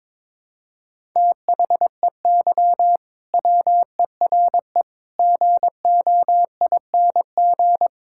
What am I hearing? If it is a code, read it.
Morse code, heard as a keyed tone, about 21 wpm, 700 Hz. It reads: THEY WERE GOING
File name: practice / 22wpm / Sentences with 3-Word Phrases - 22wpm